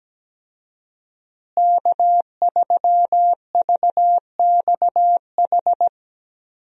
Morse code, heard as a keyed tone, about 17 wpm, 700 Hz. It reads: K3VXH
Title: USA Call Signs - 5 Characters - Repeated 2x - 17wpm